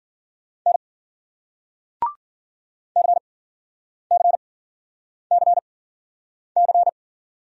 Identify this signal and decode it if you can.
Morse code, heard as a keyed tone, about 56 wpm, spaced as for 9 wpm, 700 Hz. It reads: N CCCC